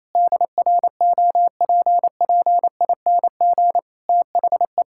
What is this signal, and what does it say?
Morse code, heard as a keyed tone, about 28 wpm, 700 Hz. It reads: DROPPING THE
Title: Pangram Challenge v2 - 28wpm